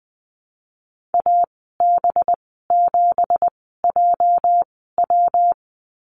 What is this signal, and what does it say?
Morse code, heard as a keyed tone, about 20 wpm, spaced as for 15 wpm, 700 Hz. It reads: AB7JW
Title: 238 Lesson 38g - Letter q - Call Signs 20wpm